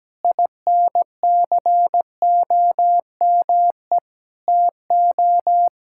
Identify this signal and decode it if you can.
Morse code, heard as a keyed tone, about 17 wpm, 700 Hz. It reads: INCOME TO
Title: Sentences from Top 1000 Words - Review Part 5 - 17wpm